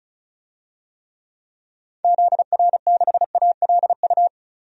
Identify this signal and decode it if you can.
Morse code, heard as a keyed tone, about 35 wpm, 700 Hz. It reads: ZR6ALU
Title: DXCC-100 Call Signs - 6 Characters - Repeated 3x - 35wpm